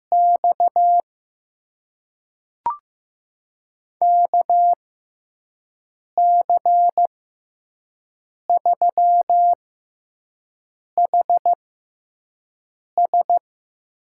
Morse code, heard as a keyed tone, about 15 wpm, spaced as for 5 wpm, 700 Hz. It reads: X KC3HS
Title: USA Call Signs - 6 Characters - 6x Character Spacing - 15wpm